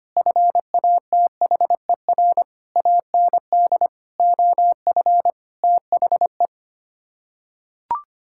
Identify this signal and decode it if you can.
Morse code, heard as a keyed tone, about 25 wpm, 700 Hz. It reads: FATHER AND OF THE
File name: Sets of 9 Words - 25wpm